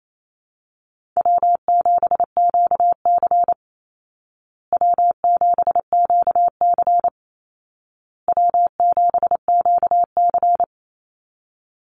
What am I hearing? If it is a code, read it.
Morse code, heard as a keyed tone, about 28 wpm, 700 Hz. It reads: W7QC W7QC W7QC
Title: Call Signs Repeated 2x - 28wpm